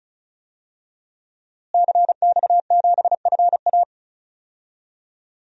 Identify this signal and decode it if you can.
Morse code, heard as a keyed tone, about 34 wpm, 700 Hz. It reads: CX7FA
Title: DXCC-100 Call Signs - 5 Characters - Repeated 3x - 35wpm